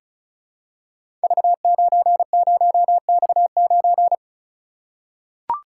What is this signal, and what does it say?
Morse code, heard as a keyed tone, about 36 wpm, 700 Hz. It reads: V90X9
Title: Code Groups - 5 Letter:Numbers - 35wpm